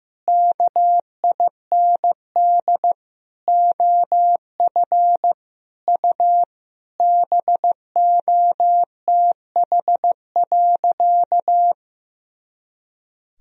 Morse code, heard as a keyed tone, about 15 wpm, 700 Hz. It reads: KIND OF U BOTH.
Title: Sentences from Top 300 Words and Abbreviations 15wpm